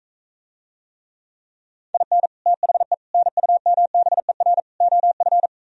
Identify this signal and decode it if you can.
Morse code, heard as a keyed tone, about 45 wpm, 700 Hz. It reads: IN THE NUMBER OF